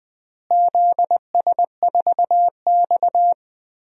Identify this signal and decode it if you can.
Morse code, heard as a keyed tone, about 21 wpm, 700 Hz. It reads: ZS4X
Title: DXCC-100 Call Signs - 4 Characters - Repeated 3x - 20wpm